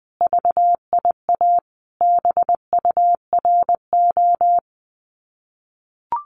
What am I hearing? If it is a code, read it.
Morse code, heard as a keyed tone, about 20 wpm, 700 Hz. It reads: VIA BURO
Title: Common QSO Phrases 20wpm